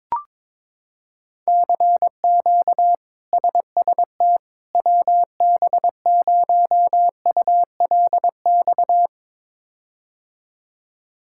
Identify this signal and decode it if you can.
Morse code, heard as a keyed tone, about 22 wpm, 700 Hz. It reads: CQ SST WB0ULX
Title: K1USN SST QSO Practice - 22wpm